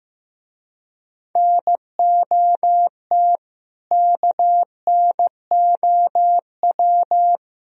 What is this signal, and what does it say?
Morse code, heard as a keyed tone, about 15 wpm, 700 Hz. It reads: NOT KNOW
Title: Speed-Racing - Sets of 4 Words 15wpm